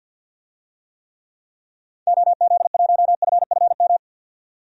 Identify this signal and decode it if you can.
Morse code, heard as a keyed tone, about 50 wpm, 700 Hz. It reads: OZ1FFM
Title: DXCC-100 Call Signs - 6 Characters - 50wpm